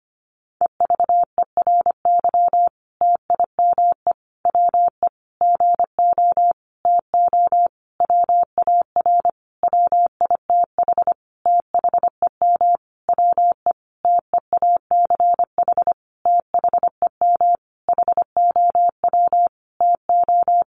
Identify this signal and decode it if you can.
Morse code, heard as a keyed tone, about 25 wpm, 700 Hz. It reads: EVERY TIME WE GO TO WAR WITH THEM WE TEACH THEM HOW TO
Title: Sentences from Top 1300 Words 25wpm